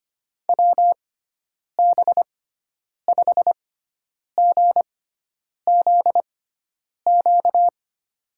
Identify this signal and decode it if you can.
Morse code, heard as a keyed tone, about 25 wpm, spaced as for 9 wpm, 700 Hz. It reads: WB5GZQ
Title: USA Call Signs - 6 Characters - 6x Character Spacing - 25wpm